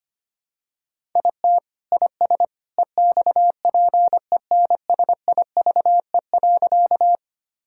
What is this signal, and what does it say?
Morse code, heard as a keyed tone, about 25 wpm, 700 Hz. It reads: IT IS EXPENSIVE.
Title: Sentences from Top 1000 Words - Review Part 6 - 25wpm